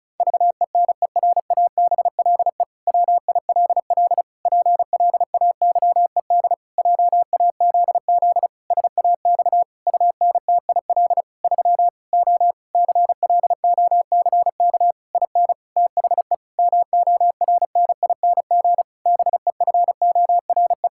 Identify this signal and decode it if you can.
Morse code, heard as a keyed tone, about 36 wpm, 700 Hz. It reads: VENERABLE WILL PLAYED JAZZ SAX UNTIL 3 O CLOCK IN THE MORNING BEFORE
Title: Pangram Challenge v2 - 35wpm